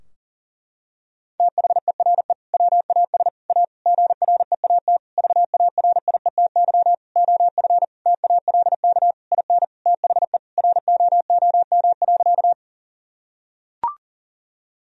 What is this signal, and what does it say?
Morse code, heard as a keyed tone, about 40 wpm, 700 Hz. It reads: THERE WAS A GREAT VARIETY OF TALK IN THE ROOM.
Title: Sentences from Top 1000 Words - Review Part 4 - 40wpm